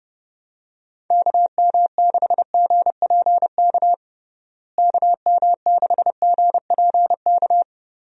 Morse code, heard as a keyed tone, about 32 wpm, 700 Hz. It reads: KM6GPK KM6GPK
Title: USA Call Signs - 6 Characters - Repeated 2x - 30wpm